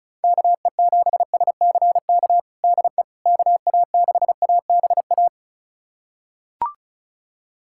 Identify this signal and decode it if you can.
Morse code, heard as a keyed tone, about 36 wpm, 700 Hz. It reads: KE7SCK DE KA6ABA
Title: USA Callsign DE Callsign - 6 Characters - 35wpm